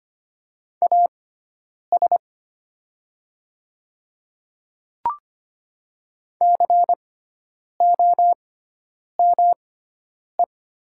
Morse code, heard as a keyed tone, about 25 wpm, spaced as for 9 wpm, 700 Hz. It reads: AS COME
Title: 6x Character Spacing - Top 100 Words - 25wpm